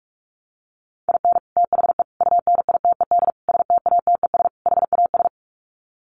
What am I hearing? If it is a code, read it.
Morse code, heard as a keyed tone, about 44 wpm, 700 Hz. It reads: IN THE UNITED STATES HAS
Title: Sets of 5 Words Encore2 45wpm